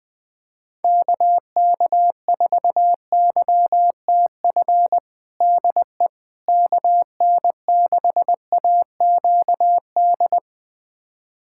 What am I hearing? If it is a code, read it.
Morse code, heard as a keyed tone, about 20 wpm, 700 Hz. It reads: KK4YTF DE KN6AQD
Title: USA Callsign DE Callsign - 6 Characters - 20wpm